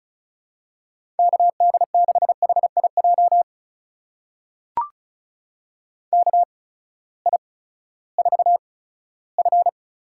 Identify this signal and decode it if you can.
Morse code, heard as a keyed tone, about 36 wpm, spaced as for 18 wpm, 700 Hz. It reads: KD6HIJ KI4F
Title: USA Call Signs - 8x Character Spacing - 35wpm